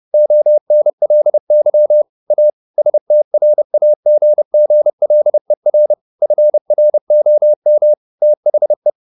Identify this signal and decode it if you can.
Morse code, heard as a keyed tone, about 30 wpm, 600 Hz. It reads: ONLY A STRAGGLER FROM THE
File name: the old man and the sea / the old man and the sea 1 30wpm